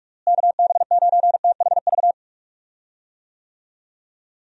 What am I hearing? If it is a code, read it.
Morse code, heard as a keyed tone, about 46 wpm, 700 Hz. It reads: KB9THV
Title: USA Call Signs - 6 Characters - Repeated 2x - 45wpm